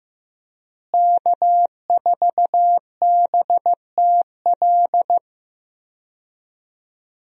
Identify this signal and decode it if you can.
Morse code, heard as a keyed tone, about 16 wpm, 700 Hz. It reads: K4BTL
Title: USA Call Signs - 5 Characters - Repeated 3x - 15wpm